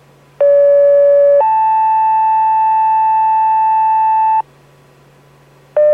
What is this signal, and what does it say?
Two-tone page, heard as a one-second tone, then a three-second tone, about 570 then 880 Hz, with noise. It is heard from 0.4 to 4.4 s.